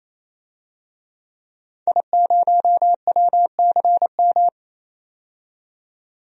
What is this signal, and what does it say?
Morse code, heard as a keyed tone, about 28 wpm, 700 Hz. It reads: I0WCM
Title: DXCC-100 Call Signs - 5 Characters - 28wpm